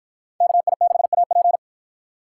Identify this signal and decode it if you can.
Morse code, heard as a keyed tone, about 53 wpm, 700 Hz. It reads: KI6AP